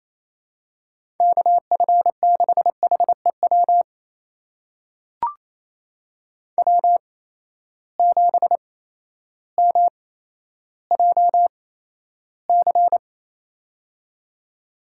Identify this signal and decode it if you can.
Morse code, heard as a keyed tone, about 28 wpm, spaced as for 13 wpm, 700 Hz. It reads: KF6HEW W7MJC
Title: USA Call Signs - 8x Character Spacing - 28wpm